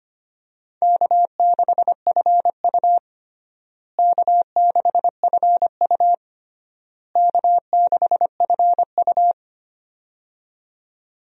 Morse code, heard as a keyed tone, about 25 wpm, 700 Hz. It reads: K6FU K6FU K6FU
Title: USA Call Signs - 4 Characters - Repeated 3x - 25wpm